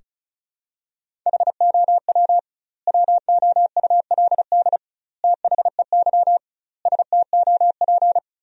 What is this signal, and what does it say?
Morse code, heard as a keyed tone, about 36 wpm, 700 Hz. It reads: HOW WOULD THEY STOP
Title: 2x Word-Spacing - Sentences from Top 300 Words - 35wpm